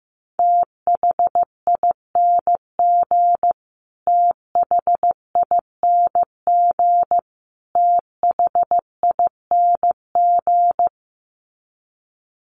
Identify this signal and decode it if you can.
Morse code, heard as a keyed tone, about 15 wpm, 700 Hz. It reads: THING THING THING
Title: Top 500 Words Repeated 15wpm